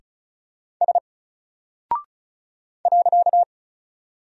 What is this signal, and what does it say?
Morse code, heard as a keyed tone, about 36 wpm, 700 Hz. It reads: S .
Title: Single Character (Letter:Number:Punctuation:Prosign) 35wpm